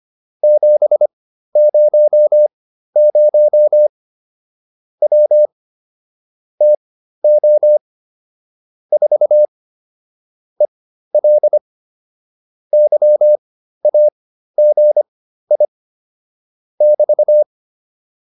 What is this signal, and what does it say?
Morse code, heard as a keyed tone, about 25 wpm, spaced as for 13 wpm, 600 Hz. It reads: 700 W TO 4 EL YAGI <BT>
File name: QSO202 13